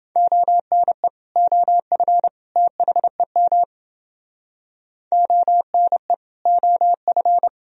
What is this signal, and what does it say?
Morse code, heard as a keyed tone, about 29 wpm, 700 Hz. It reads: ONE OF THEM ONE OF